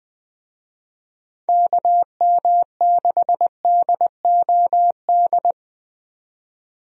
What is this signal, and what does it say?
Morse code, heard as a keyed tone, about 20 wpm, 700 Hz. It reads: KM6DOD
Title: USA Call Signs - 6 Characters - 20wpm